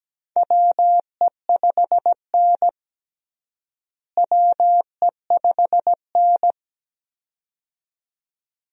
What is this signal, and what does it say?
Morse code, heard as a keyed tone, about 17 wpm, 700 Hz. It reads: WE5N WE5N
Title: USA Call Signs - 4 Characters - Repeated 2x - 17wpm